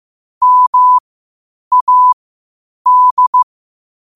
Morse code, heard as a keyed tone, about 15 wpm, spaced as for 8 wpm, 1000 Hz. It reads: MAD
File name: qotdfiles / 8wpm